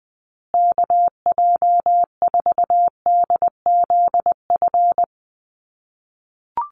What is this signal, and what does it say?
Morse code, heard as a keyed tone, about 20 wpm, 700 Hz. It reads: KJ4DZF
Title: USA Call Signs Repeated 2x 20wpm